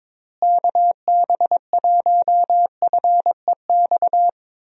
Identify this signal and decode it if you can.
Morse code, heard as a keyed tone, about 21 wpm, 700 Hz. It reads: KB1FEX
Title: USA Call Signs - 6 Characters - Repeated 2x - 22wpm